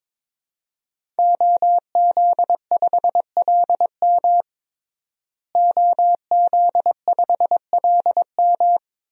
Morse code, heard as a keyed tone, about 22 wpm, 700 Hz. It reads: OZ5LM OZ5LM
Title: DXCC-100 Call Signs - 5 Characters - Repeated 3x - 22wpm